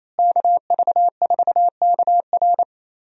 Morse code, heard as a keyed tone, about 27 wpm, 700 Hz. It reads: KV4KR
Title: USA Call Signs - 5 Characters - Repeated 3x - 28wpm